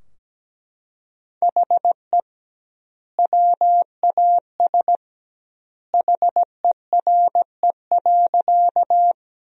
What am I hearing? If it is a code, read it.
Morse code, heard as a keyed tone, about 17 wpm, 700 Hz. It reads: HE WAS HERE.
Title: 2x Word-Spacing - Sentences from Top 200 Words - 17wpm